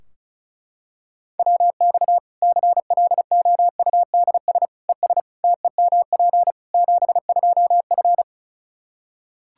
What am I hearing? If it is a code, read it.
Morse code, heard as a keyed tone, about 34 wpm, 700 Hz. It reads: WX CLOUDS ES TEMP 72F
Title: Practice QSO - Exchange 2 - Info - 35wpm